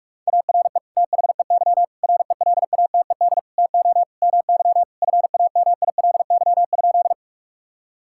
Morse code, heard as a keyed tone, about 41 wpm, 700 Hz. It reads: ARE THEY RELATED TO MY FAMILY?
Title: Sentences from Top 1000 Words - Review Part 5 - 45wpm